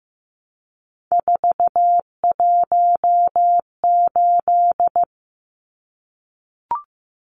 Morse code, heard as a keyed tone, about 15 wpm, 700 Hz. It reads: 418
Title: Numbers - 3-Digit 15wpm